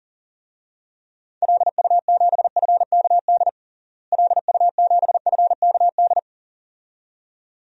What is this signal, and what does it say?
Morse code, heard as a keyed tone, about 40 wpm, 700 Hz. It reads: LU7FKD LU7FKD
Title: DXCC-100 Call Signs - 6 Characters - Repeated 2x - 40wpm